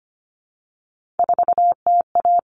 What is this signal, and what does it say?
Morse code, heard as a keyed tone, about 25 wpm, 700 Hz. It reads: 4TA
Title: All Cut Numbers - 3-Digit 25wpm